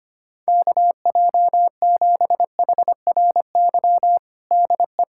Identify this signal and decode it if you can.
Morse code, heard as a keyed tone, about 25 wpm, 700 Hz. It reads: KJ7HRY DE
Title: USA Callsign DE Callsign 25wpm